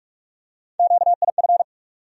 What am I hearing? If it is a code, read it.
Morse code, heard as a keyed tone, about 46 wpm, 700 Hz. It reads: QIF
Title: USA Call Sign Postfixes - 3 Letters 45wpm